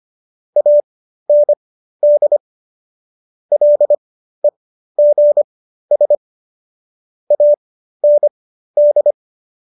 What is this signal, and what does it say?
Morse code, heard as a keyed tone, about 25 wpm, spaced as for 13 wpm, 590 Hz. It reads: AND LEGS AND